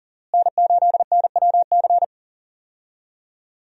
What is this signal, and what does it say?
Morse code, heard as a keyed tone, about 40 wpm, 700 Hz. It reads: N8NWC
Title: USA Call Signs - 5 Characters - Repeated 3x - 40wpm